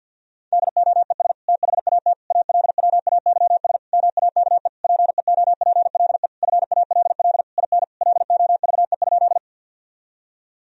Morse code, heard as a keyed tone, about 52 wpm, 700 Hz. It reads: DOES THAT ALWAYS MAKE PEOPLE FALL IN LOVE?
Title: Sentences from Top 1000 Words - Review Part 2 - 50wpm